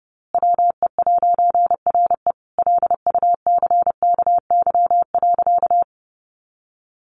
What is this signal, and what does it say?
Morse code, heard as a keyed tone, about 30 wpm, 700 Hz. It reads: WE'RE LUCKY.
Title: A Sound Of Thunder 30wpm